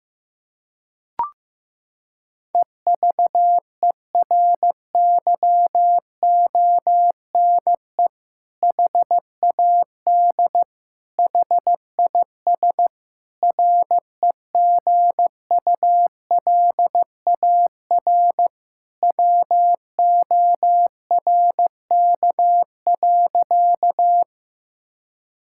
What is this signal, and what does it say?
Morse code, heard as a keyed tone, about 15 wpm, 700 Hz. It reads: EVERYONE HAD HIS REGULAR WORK.
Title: Sentences from Top 1000 Words - Review Part 6 - 15wpm